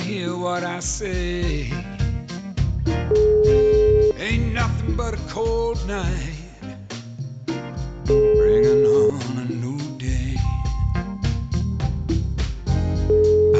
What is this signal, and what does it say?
Ringback tone (Europe, 425 Hz), with speech.